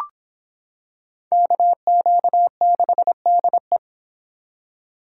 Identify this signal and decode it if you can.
Morse code, heard as a keyed tone, about 26 wpm, 700 Hz. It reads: KQ6DE